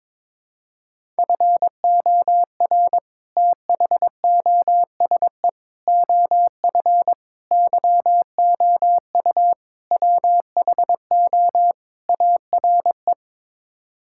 Morse code, heard as a keyed tone, about 22 wpm, 700 Hz. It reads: FOR THOSE OF YOU WHO ARE
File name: practice / 22wpm / Sets of 6 Words - 22wpm